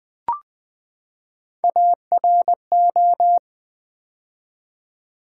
Morse code, heard as a keyed tone, about 20 wpm, 700 Hz. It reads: ARO
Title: Build A Word - 3 Letters At A Time - 6-Letter Words - 20wpm